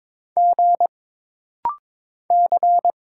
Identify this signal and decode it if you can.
Morse code, heard as a keyed tone, about 22 wpm, 700 Hz. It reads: G C
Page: This rapid-fire timing is geared to push your ICR (Instant Character Recognition) skills.